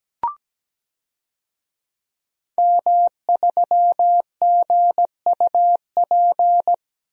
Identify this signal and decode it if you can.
Morse code, heard as a keyed tone, about 17 wpm, 700 Hz. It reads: M3GUP
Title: DXCC-100 Call Signs - 5 Characters - Repeated 2x - 17wpm